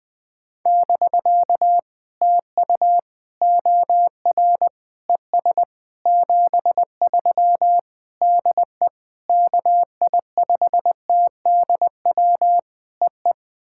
Morse code, heard as a keyed tone, about 20 wpm, 700 Hz. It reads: <BK> TU OR ES 73 DE KI5TDW EE
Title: POTA QSO Practice - 20wpm